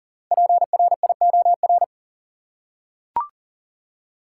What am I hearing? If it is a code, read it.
Morse code, heard as a keyed tone, about 37 wpm, 700 Hz. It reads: PRIOR
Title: Five Letter Words - 40wpm